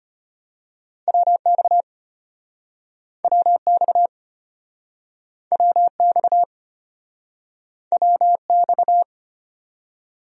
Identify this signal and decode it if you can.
Morse code, heard as a keyed tone, about 35 wpm, 700 Hz. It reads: WX WX WX WX